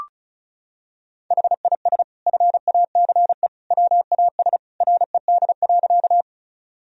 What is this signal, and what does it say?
Morse code, heard as a keyed tone, about 36 wpm, 700 Hz. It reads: HIS FACE WAS RED.
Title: Sentences from Top 1000 Words - Review Part 2 - 35wpm